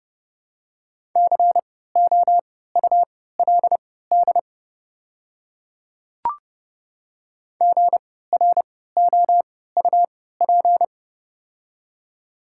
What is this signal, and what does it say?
Morse code, heard as a keyed tone, about 32 wpm, spaced as for 17 wpm, 700 Hz. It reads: COULD GROUP
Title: 3x Character Spacing - Top 300 Words - 30wpm